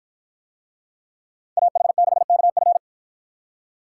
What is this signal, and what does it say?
Morse code, heard as a keyed tone, about 52 wpm, 700 Hz. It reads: AH6KF